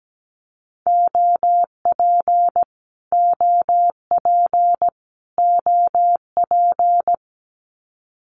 Morse code, heard as a keyed tone, about 17 wpm, 700 Hz. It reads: OP OP OP
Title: Abbreviations Repeated 3x - 17wpm